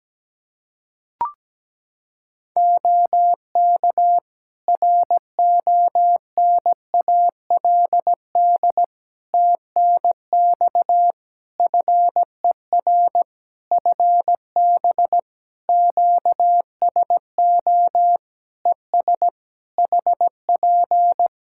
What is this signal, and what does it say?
Morse code, heard as a keyed tone, about 17 wpm, 700 Hz. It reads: OK RONALD TNX FER FB QSO ES HP
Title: Practice QSO - Exchange 4 - Ending - 17wpm